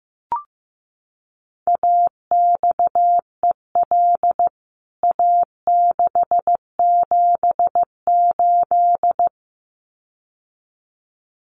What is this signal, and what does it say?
Morse code, heard as a keyed tone, about 15 wpm, 700 Hz. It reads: AXEL A678
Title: CWT Exchange 15wpm